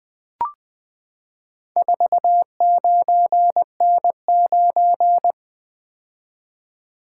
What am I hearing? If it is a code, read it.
Morse code, heard as a keyed tone, about 20 wpm, 700 Hz. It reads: 49N9
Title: Four Letter-Number Combinations - 20wpm